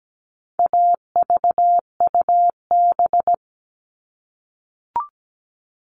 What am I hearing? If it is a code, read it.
Morse code, heard as a keyed tone, about 17 wpm, 700 Hz. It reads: AVUB
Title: All Cut Numbers - 4-Digit 17wpm